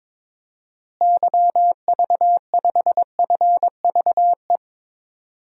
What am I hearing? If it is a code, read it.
Morse code, heard as a keyed tone, about 22 wpm, 700 Hz. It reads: YV5FVE
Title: DXCC-100 Call Signs - 6 Characters - Repeated 3x - 22wpm